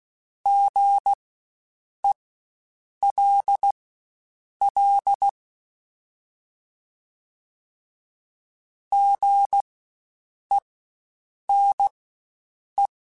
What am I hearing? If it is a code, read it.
Morse code, heard as a keyed tone, about 16 wpm, spaced as for 7 wpm, 800 Hz. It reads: GELL GENE
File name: Spaced Alphas / Spaced Alphas mp3s / Gs sp @15